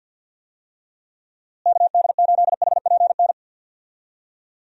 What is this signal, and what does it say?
Morse code, heard as a keyed tone, about 48 wpm, 700 Hz. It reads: KD8HPN
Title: USA Call Signs - 6 Characters - Repeated 3x - 50wpm